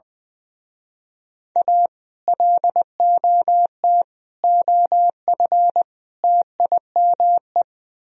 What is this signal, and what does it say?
Morse code, heard as a keyed tone, about 20 wpm, 700 Hz. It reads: A LOT OF TIME
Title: Speed-Racing - Sets of 4 Words 20wpm